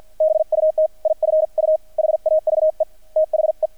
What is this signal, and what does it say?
Morse code, heard as a keyed tone, about 50 wpm, 630 Hz. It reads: BUT IF U HAVE THE
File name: archived / CRSnet